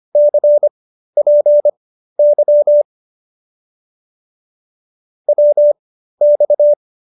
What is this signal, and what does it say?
Morse code, heard as a keyed tone, about 25 wpm, spaced as for 13 wpm, 590 Hz. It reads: CPY WX